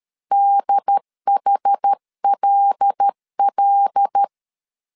Morse code, heard as a keyed tone, about 13 wpm, 800 Hz. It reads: DHLL
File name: CW / DET / 1 6 1500C S A 12wpm 3440